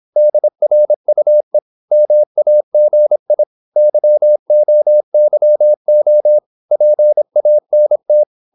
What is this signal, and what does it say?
Morse code, heard as a keyed tone, about 26 wpm, 600 Hz. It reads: DRUE MAGI YOYO PANT
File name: Korte ord DK 26wpm